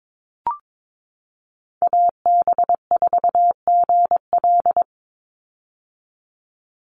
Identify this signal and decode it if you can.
Morse code, heard as a keyed tone, about 22 wpm, 700 Hz. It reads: AB4GL
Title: USA Call Signs 22wpm